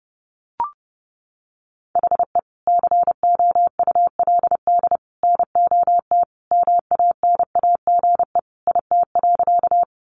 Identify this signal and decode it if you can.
Morse code, heard as a keyed tone, about 30 wpm, 700 Hz. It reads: HE COULD NOT MANAGE IT.
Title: Sentences from Top 1400 Words 30wpm